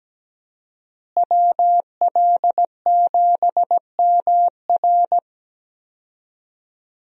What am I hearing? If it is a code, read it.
Morse code, heard as a keyed tone, about 17 wpm, 700 Hz. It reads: WL7MR
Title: USA Call Signs - 5 Characters - Repeated 3x - 17wpm